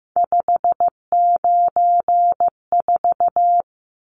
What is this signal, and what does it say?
Morse code, heard as a keyed tone, about 15 wpm, 700 Hz. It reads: 594
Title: Numbers - 3-Digit 15wpm